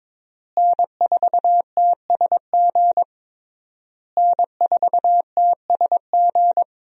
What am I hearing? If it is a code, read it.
Morse code, heard as a keyed tone, about 21 wpm, 700 Hz. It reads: N4TSG N4TSG
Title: USA Call Signs - 5 Characters - Repeated 3x - 22wpm